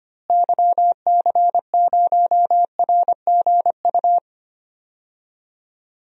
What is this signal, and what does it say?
Morse code, heard as a keyed tone, about 25 wpm, 700 Hz. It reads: YC0RGU
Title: DXCC-100 Call Signs - 6 Characters - Repeated 2x - 25wpm